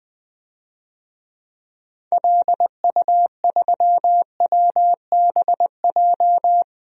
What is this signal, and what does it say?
Morse code, heard as a keyed tone, about 21 wpm, 700 Hz. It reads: LU3WBJ
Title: DXCC-100 Call Signs - 6 Characters - Repeated 3x - 20wpm